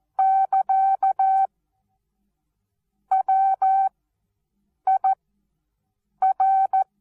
Morse code, heard as a keyed tone, about 14 wpm, spaced as for 7 wpm, 770 Hz. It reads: <KA> WIR
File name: Onlinelehrgang / morsen / DJ4UF Morsen19